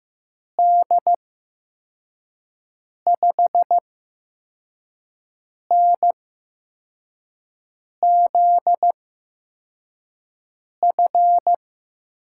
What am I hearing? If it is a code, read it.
Morse code, heard as a keyed tone, about 15 wpm, spaced as for 4 wpm, 700 Hz. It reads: D5NZF
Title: USA Call Signs - 6 Characters - 8x Character Spacing - 15wpm